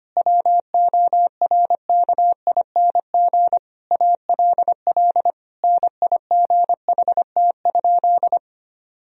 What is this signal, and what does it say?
Morse code, heard as a keyed tone, about 25 wpm, 700 Hz. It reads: WORKING ALL NIGHT?
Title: Sentences from Top 1000 Words - Review Part 4 - 25wpm